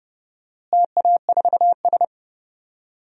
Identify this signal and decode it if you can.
Morse code, heard as a keyed tone, about 30 wpm, 700 Hz. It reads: TA4S
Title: DXCC-100 Call Signs - 4 Characters - Repeated 3x - 30wpm